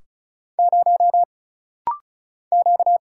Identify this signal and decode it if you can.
Morse code, heard as a keyed tone, about 35 wpm, 700 Hz. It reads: 0 Q
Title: Single Letter-Number - Rapid-Fire 35wpm